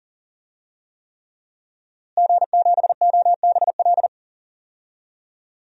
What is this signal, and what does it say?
Morse code, heard as a keyed tone, about 40 wpm, 700 Hz. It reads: G7OBL
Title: DXCC-100 Call Signs - 5 Characters - Repeated 3x - 40wpm